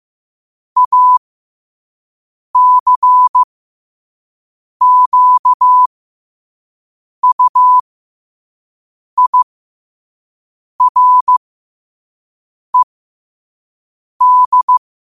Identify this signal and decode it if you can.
Morse code, heard as a keyed tone, about 15 wpm, spaced as for 5 wpm, 1000 Hz. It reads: ACQUIRED
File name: qotdfiles / 5wpm